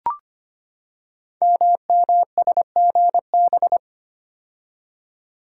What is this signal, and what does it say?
Morse code, heard as a keyed tone, about 25 wpm, 700 Hz. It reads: MMSGB
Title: Code Groups - 5 Letters - 25wpm